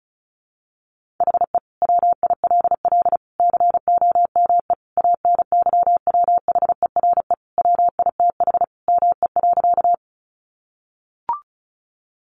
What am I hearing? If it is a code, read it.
Morse code, heard as a keyed tone, about 34 wpm, 700 Hz. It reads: HE WILL COME ANYWHERE WITH ME.
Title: Sentences from Top 1400 Words 35wpm